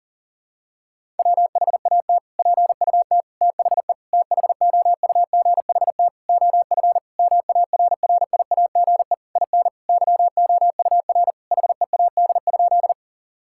Morse code, heard as a keyed tone, about 40 wpm, 700 Hz. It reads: WHAT PUT THE THOUGHT OF MARRIAGE IN YOUR HEAD?
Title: Sentences from Top 1000 Words - Review Part 5 - 40wpm